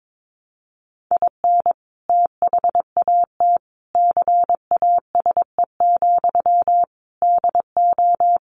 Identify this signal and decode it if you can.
Morse code, heard as a keyed tone, about 22 wpm, 700 Hz. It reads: IN THAT CASE, DO
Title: Sentences from Top 1300 Words 22wpm